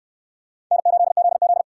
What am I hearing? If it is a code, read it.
Morse code, heard as a keyed tone, about 66 wpm, 700 Hz. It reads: N8ZZ